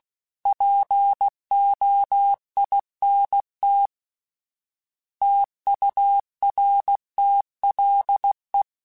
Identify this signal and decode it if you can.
Morse code, heard as a keyed tone, about 16 wpm, 800 Hz. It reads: POINT TURTLE
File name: fromSMs / FromSMs mp3s / Cities@15